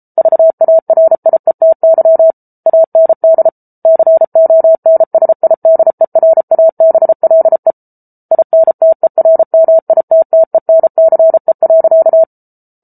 Morse code, heard as a keyed tone, about 34 wpm, 650 Hz. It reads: VARIETY AND CONSIDERABLE INTERMITTENCE.